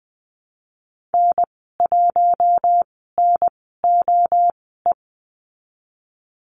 Morse code, heard as a keyed tone, about 20 wpm, spaced as for 15 wpm, 700 Hz. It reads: N1NOE
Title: 048 Lesson 9g - Number 4 - Call Signs 20wpm